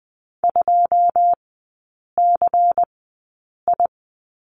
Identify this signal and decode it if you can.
Morse code, heard as a keyed tone, about 20 wpm, spaced as for 8 wpm, 690 Hz. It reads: 2CI
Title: 093 Lesson 16e - Letter c - Three Characters 20wpm